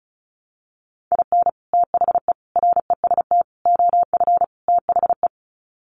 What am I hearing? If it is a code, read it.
Morse code, heard as a keyed tone, about 34 wpm, 700 Hz. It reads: IN THE REST OF THE
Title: Sets of 5 Words 35wpm